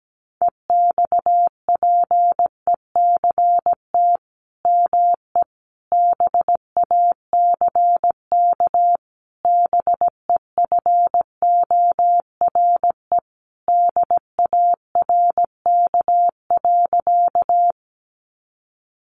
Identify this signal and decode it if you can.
Morse code, heard as a keyed tone, about 17 wpm, 700 Hz. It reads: EXPECT ME BACK BEFORE DARK.
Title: Sentences from Top 1100 Words 17wpm